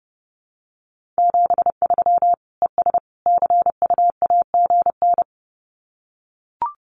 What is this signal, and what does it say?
Morse code, heard as a keyed tone, about 30 wpm, 700 Hz. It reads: 73 ES CUAGN
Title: Common QSO Phrases 30wpm